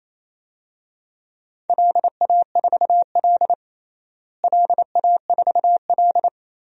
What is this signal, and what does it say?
Morse code, heard as a keyed tone, about 28 wpm, 700 Hz. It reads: LA4L LA4L
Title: DXCC-100 Call Signs - 4 Characters - Repeated 3x - 28wpm